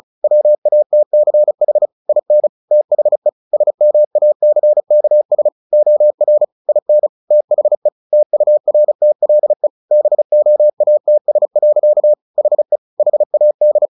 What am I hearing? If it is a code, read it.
Morse code, heard as a keyed tone, about 34 wpm, 600 Hz. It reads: WATCH IN THE SMACKS OR IN THE TURTLE BOATS. HE HAD
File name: the old man and the sea / the old man and the sea 1 35wpm